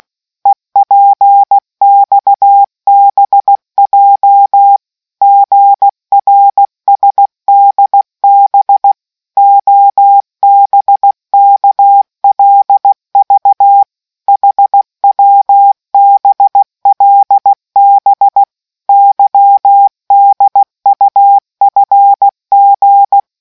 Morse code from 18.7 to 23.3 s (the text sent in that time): YDUFG